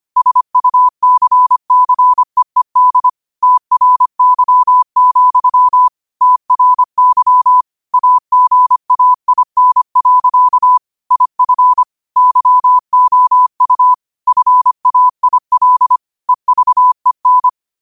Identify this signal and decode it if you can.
Morse code, heard as a keyed tone, about 25 wpm, 1000 Hz. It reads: SUCCEED TRY, TRY AGAIN. IF YOU FAIL EVEN
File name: qotdfiles / 25wpm